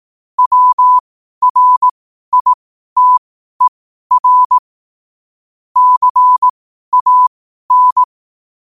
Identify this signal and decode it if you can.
Morse code, heard as a keyed tone, about 18 wpm, spaced as for 12 wpm, 1000 Hz. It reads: WRITER CAN
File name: qotdfiles / 12wpm